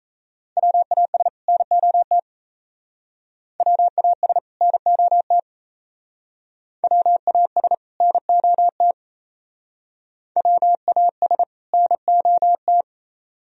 Morse code, heard as a keyed tone, about 35 wpm, 700 Hz. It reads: WAS NOT WAS NOT WAS NOT WAS NOT